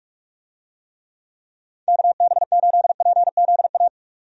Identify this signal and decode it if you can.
Morse code, heard as a keyed tone, about 46 wpm, 700 Hz. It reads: KB8PZA
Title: USA Call Signs - 6 Characters - 45wpm